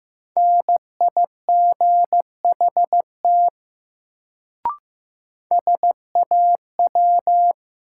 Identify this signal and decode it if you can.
Morse code, heard as a keyed tone, about 15 wpm, 700 Hz. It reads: NIGHT SAW
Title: Sentences from Top 300 Words - New Word Drill - 15wpm